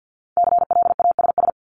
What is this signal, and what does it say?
Morse code, heard as a keyed tone, about 50 wpm, 700 Hz. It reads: CLASS
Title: Top 500 Words 50wpm